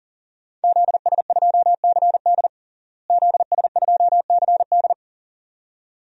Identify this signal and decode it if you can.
Morse code, heard as a keyed tone, about 40 wpm, 700 Hz. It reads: ZS2CD ZS2CD
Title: DXCC-100 Call Signs - 5 Characters - Repeated 2x - 40wpm